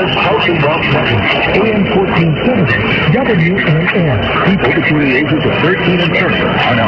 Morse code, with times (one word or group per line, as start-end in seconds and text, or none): none